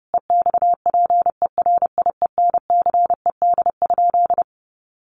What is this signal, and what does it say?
Morse code, heard as a keyed tone, about 30 wpm, 700 Hz. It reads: EXPERIENCED?
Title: Sentences from Top 1100 Words 30wpm